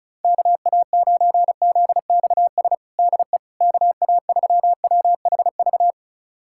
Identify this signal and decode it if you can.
Morse code, heard as a keyed tone, about 34 wpm, 700 Hz. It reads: KA9ZXS DE KA3WHV
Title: USA Callsign DE Callsign - 6 Characters - 35wpm